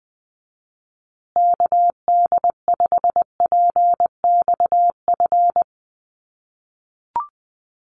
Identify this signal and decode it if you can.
Morse code, heard as a keyed tone, about 20 wpm, 700 Hz. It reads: KD5PXF
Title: USA Call Signs Repeated 3x 20wpm